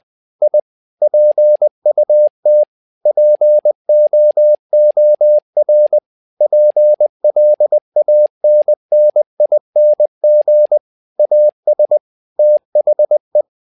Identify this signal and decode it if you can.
Morse code, heard as a keyed tone, about 20 wpm, 600 Hz. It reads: I PUT POOR PLANNING AS THE
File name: learning center / tower trouble / tower trouble 20wpm